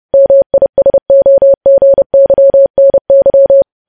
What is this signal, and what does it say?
Morse code, heard as a keyed tone, about 30 wpm, 550 Hz. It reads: MISOGYNY
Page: news headlines in morse code at 30 wpm